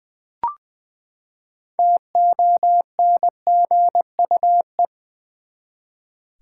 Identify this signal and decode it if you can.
Morse code, heard as a keyed tone, about 20 wpm, 700 Hz. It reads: TONGUE
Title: Six Letter Words - 20wpm